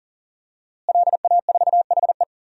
Morse code, heard as a keyed tone, about 40 wpm, 700 Hz. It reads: LA4HE
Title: DXCC-100 Call Signs - 5 Characters - Repeated 3x - 40wpm